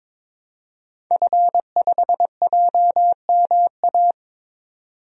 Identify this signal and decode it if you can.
Morse code, heard as a keyed tone, about 22 wpm, 700 Hz. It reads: F5JMA
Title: DXCC-100 Call Signs - 5 Characters - 22wpm